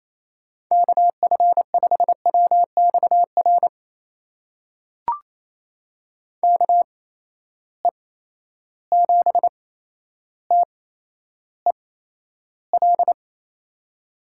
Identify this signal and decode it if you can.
Morse code, heard as a keyed tone, about 28 wpm, spaced as for 12 wpm, 700 Hz. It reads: KF5WXR KE7TEL
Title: USA Call Signs - 6 Characters - 8x Character Spacing - 28wpm